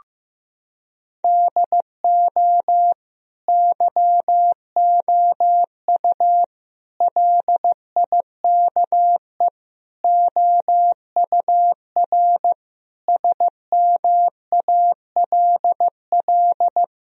Morse code, heard as a keyed tone, about 15 wpm, 700 Hz. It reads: DO YOU LIKE OUR SMALL
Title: Sentences from Top 1000 Words - Review Part 1 - 15wpm